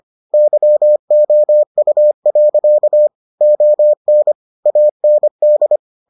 Morse code, heard as a keyed tone, about 25 wpm, 600 Hz. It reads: YOU. ON AND